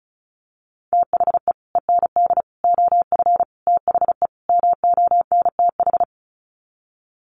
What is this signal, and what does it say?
Morse code, heard as a keyed tone, about 34 wpm, 700 Hz. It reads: THE END OF THE MONTH
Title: Sets of 5 Words Encore 35wpm